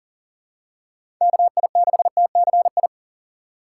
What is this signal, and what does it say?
Morse code, heard as a keyed tone, about 40 wpm, 700 Hz. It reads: KI6TCI
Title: USA Call Signs - 6 Characters - 6x Character Spacing - 40wpm